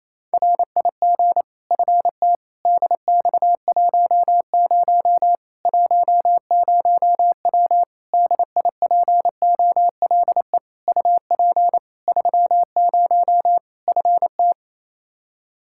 Morse code, heard as a keyed tone, about 27 wpm, 700 Hz. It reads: RIG FT DX10 10W DIPOLE UP 30 FT
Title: Practice QSO - Exchange 2 - Info - 28wpm